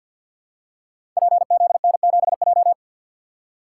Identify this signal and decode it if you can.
Morse code, heard as a keyed tone, about 48 wpm, 700 Hz. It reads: PZN7J
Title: Code Groups - 5 Letter:Numbers - 50wpm